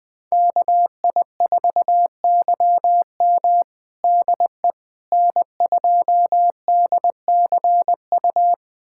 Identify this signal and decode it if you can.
Morse code, heard as a keyed tone, about 21 wpm, 700 Hz. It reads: KI4YM DE N2DCU
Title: USA Callsign DE Callsign - 5 Characters - 20wpm